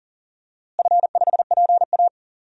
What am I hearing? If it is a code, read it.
Morse code, heard as a keyed tone, about 40 wpm, 700 Hz. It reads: F5PA
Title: DXCC-100 Call Signs - 4 Characters - Repeated 2x - 40wpm